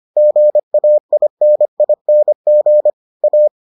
Morse code, heard as a keyed tone, about 25 wpm, 600 Hz. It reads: GAINING A